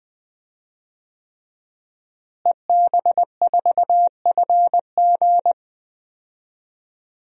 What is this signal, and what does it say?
Morse code, heard as a keyed tone, about 20 wpm, 700 Hz. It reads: EB4FG
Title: DXCC-100 Call Signs - 5 Characters - Repeated 3x - 20wpm